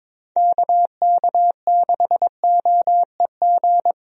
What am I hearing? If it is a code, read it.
Morse code, heard as a keyed tone, about 22 wpm, 700 Hz. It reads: KK6OEG
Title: USA Call Signs - 6 Characters - Repeated 3x - 22wpm